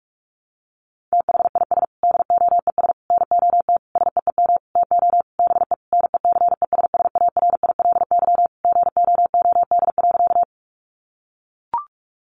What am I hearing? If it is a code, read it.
Morse code, heard as a keyed tone, about 44 wpm, 700 Hz. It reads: THIS DOES NOT SEEM TO BE NECESSARILY GOOD.
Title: Sentences from Top 1400 Words 45wpm